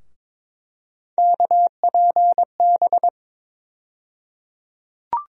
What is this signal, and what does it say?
Morse code, heard as a keyed tone, about 22 wpm, 700 Hz. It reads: KPB
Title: USA Call Sign Postfixes - 3 Letters 22wpm